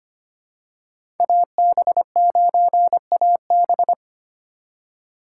Morse code, heard as a keyed tone, about 25 wpm, 700 Hz. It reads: AB9AB
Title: USA Call Signs - 5 Characters - 2x Character Spacing - 25wpm